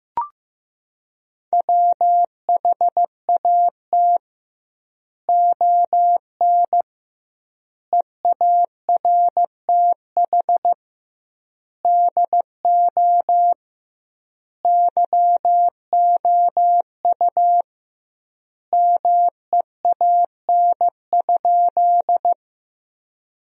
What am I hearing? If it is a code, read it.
Morse code, heard as a keyed tone, about 15 wpm, 700 Hz. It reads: WHAT ON EARTH DO YOU MEAN?
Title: 2x Word-Spacing - Sentences from Top 200 Words - 15wpm